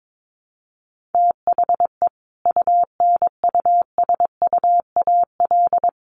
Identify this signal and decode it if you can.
Morse code, heard as a keyed tone, about 22 wpm, 700 Hz. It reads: THE UNUSUAL
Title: Sentences from Top 1400 Words 22wpm